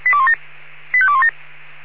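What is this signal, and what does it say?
Five-tone selective call (ZVEI), heard as quick runs of five tones, tones of about 70 ms. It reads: 85217
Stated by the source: ZVEI five-tone selective call received by SDR.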